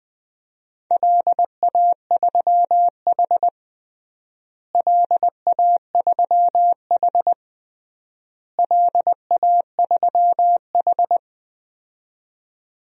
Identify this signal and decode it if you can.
Morse code, heard as a keyed tone, about 21 wpm, 700 Hz. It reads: LA3H LA3H LA3H
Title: DXCC-100 Call Signs - 4 Characters - Repeated 3x - 20wpm